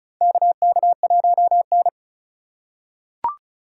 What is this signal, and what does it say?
Morse code, heard as a keyed tone, about 35 wpm, 700 Hz. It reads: KK1N
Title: USA Call Signs - 4 Characters - 3x Character Spacing - 35wpm